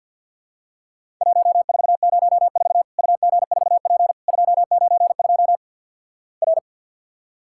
Morse code, heard as a keyed tone, about 53 wpm, 700 Hz. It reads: 1404 UG4P 292 R
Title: Worked All Europe DX Contest - QTC Exchange - 50wpm